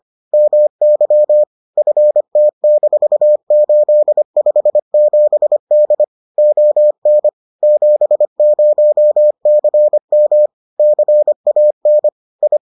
Morse code, heard as a keyed tone, about 25 wpm, 600 Hz. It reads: MY FT-857D ON 70CM CAN I